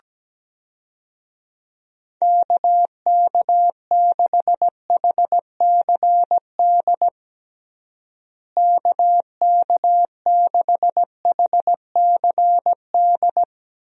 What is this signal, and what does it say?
Morse code, heard as a keyed tone, about 17 wpm, 700 Hz. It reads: KK6HCD KK6HCD
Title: USA Call Signs - 6 Characters - Repeated 3x - 17wpm